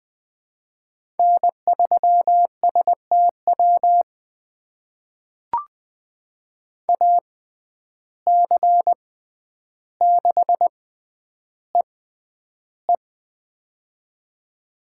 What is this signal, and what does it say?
Morse code, heard as a keyed tone, about 20 wpm, spaced as for 10 wpm, 700 Hz. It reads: N3STW AC6EE
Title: USA Call Signs - 5 Characters - 6x Character Spacing - 20wpm